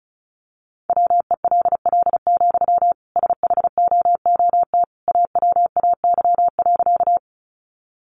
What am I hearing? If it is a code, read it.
Morse code, heard as a keyed tone, about 34 wpm, 700 Hz. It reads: WELL, SHOOT AWAY.
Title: Sentences from Top 1200 Words 35wpm